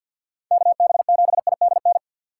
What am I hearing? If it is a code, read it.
Morse code, heard as a keyed tone, about 51 wpm, 700 Hz. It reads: KB7IDN
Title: USA Call Signs - 6 Characters - 3x Character Spacing - 50wpm